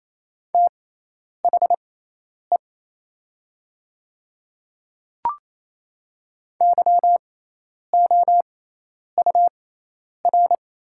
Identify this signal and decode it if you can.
Morse code, heard as a keyed tone, about 27 wpm, spaced as for 10 wpm, 700 Hz. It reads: THE YOUR
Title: 6x Character Spacing - Top 100 Words - 28wpm